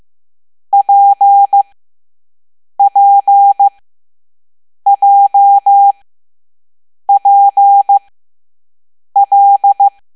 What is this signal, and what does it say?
Morse code, heard as a keyed tone, about 15 wpm, spaced as for 6 wpm, 800 Hz. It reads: PPJPL
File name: Morse / Level7B